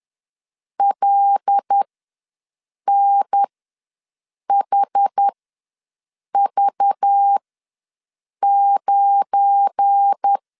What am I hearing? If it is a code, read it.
Morse code, heard as a keyed tone, about 10 wpm, spaced as for 6 wpm, 800 Hz. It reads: LNHV9
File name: LEZIONI / 8 YJ45 / 8 YJ45 ws 10wpm 3843